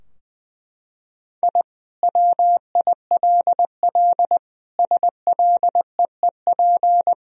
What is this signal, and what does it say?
Morse code, heard as a keyed tone, about 20 wpm, 700 Hz. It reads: I WILL SLEEP
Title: Sentences from Top 1000 Words - Review Part 2 - 20wpm